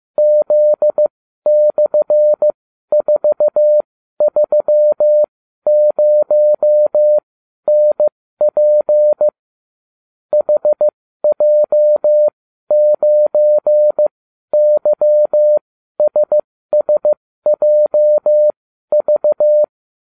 Morse code, heard as a keyed tone, about 15 wpm, spaced as for 12 wpm, 600 Hz. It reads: Z/430NP HJ9YSSJV